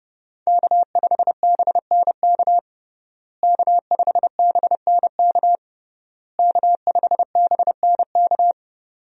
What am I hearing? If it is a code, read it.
Morse code, heard as a keyed tone, about 30 wpm, 700 Hz. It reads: K5BNK K5BNK K5BNK
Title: USA Call Signs - 5 Characters - Repeated 3x - 30wpm